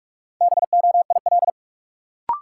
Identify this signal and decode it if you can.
Morse code, heard as a keyed tone, about 44 wpm, 700 Hz. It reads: BOIL
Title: Sets of 5 Words Encore - New Word Drill - 45wpm